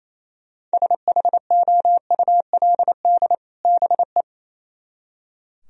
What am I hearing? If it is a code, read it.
Morse code, heard as a keyed tone, about 36 wpm, 700 Hz. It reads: SHOULD BE